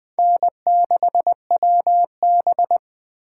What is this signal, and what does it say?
Morse code, heard as a keyed tone, about 20 wpm, 700 Hz. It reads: N6WB
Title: Speed Racing - USA 1x2 Call Signs 17wpm